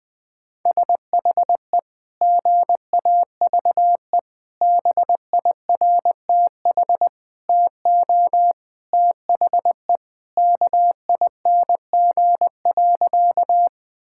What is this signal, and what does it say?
Morse code, heard as a keyed tone, about 21 wpm, 700 Hz. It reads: SHE GAVE BIRTH TO THE KING.
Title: Sentences from Top 1000 Words - Review Part 5 - 20wpm